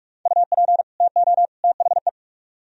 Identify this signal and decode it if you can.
Morse code, heard as a keyed tone, about 60 wpm, spaced as for 51 wpm, 700 Hz. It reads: UP TO THE
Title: Speed-Racing - Sets of 3 Words 45wpm